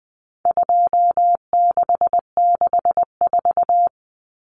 Morse code, heard as a keyed tone, about 20 wpm, 700 Hz. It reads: 2664
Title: Common Cut Numbers - 4-Digit 20wpm